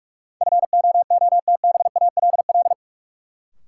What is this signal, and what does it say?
Morse code, heard as a keyed tone, about 45 wpm, 700 Hz. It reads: FOOTBALL
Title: Sentences from Top 1600 Words - New Word Drill Alt - 45wpm